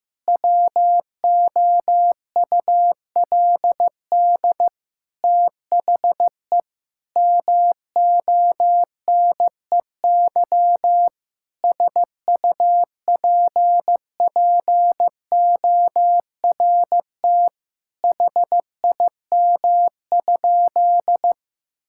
Morse code, heard as a keyed tone, about 16 wpm, 700 Hz. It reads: WOULD THE MONEY SUPPORT HIM?
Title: Sentences from Top 1000 Words - Review Part 5 - 15wpm